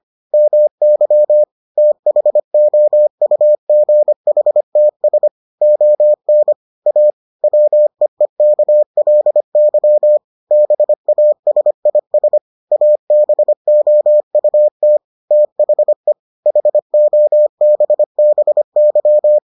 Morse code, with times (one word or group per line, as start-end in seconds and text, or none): 0.3-1.4 MY
1.8-5.3 THOUGHTS
5.6-6.5 ON
6.9-7.1 A
7.4-10.2 WEEKLY
10.5-12.4 BASIS
12.7-15.0 ABOUT
15.3-16.1 THE
16.5-19.4 HOBBY